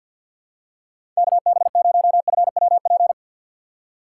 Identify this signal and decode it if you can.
Morse code, heard as a keyed tone, about 51 wpm, 700 Hz. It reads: KB0FWP
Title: USA Call Signs - 4x Character Spacing - 50wpm